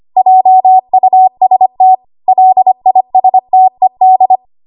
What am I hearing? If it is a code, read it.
Morse code, heard as a keyed tone, about 25 wpm, 750 Hz. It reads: JUST LISTED
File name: Archive / 25 WPM / 25WPM